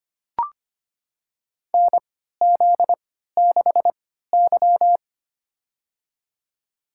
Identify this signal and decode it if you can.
Morse code, heard as a keyed tone, about 25 wpm, spaced as for 14 wpm, 700 Hz. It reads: NZ6Y
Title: USA Call Signs - 4 Characters - 3x Character Spacing - 25wpm